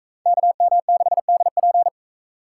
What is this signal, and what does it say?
Morse code, heard as a keyed tone, about 41 wpm, 700 Hz. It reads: KM6DP